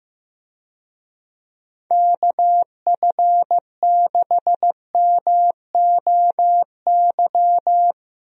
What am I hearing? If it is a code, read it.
Morse code, heard as a keyed tone, about 15 wpm, 700 Hz. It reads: KF6MOY
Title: USA Call Signs - 6 Characters - 15wpm